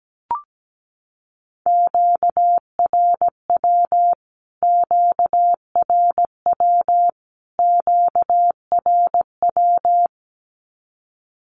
Morse code, heard as a keyed tone, about 17 wpm, 700 Hz. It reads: QRW QRW QRW
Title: Q-Codes Repeated 3x 17wpm